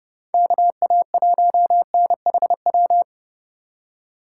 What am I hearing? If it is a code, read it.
Morse code, heard as a keyed tone, about 30 wpm, 700 Hz. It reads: KA1NHW